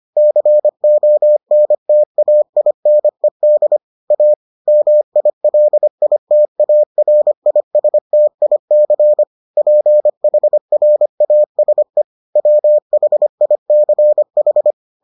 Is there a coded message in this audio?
Yes: Morse code from 0.2 to 14.7 s, heard as a keyed tone, about 25 wpm, 600 Hz. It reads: CONTAINED A MILITARISTIC PHRASE WHICH